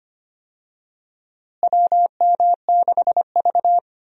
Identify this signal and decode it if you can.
Morse code, heard as a keyed tone, about 25 wpm, 700 Hz. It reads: WM6V
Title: USA Call Signs - 4 Characters - Repeated 2x - 25wpm